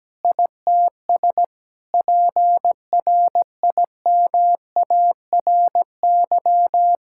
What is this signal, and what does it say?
Morse code, heard as a keyed tone, about 17 wpm, 700 Hz. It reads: ITS PRIMARY
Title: Sentences from Top 1000 Words - Review Part 5 - 17wpm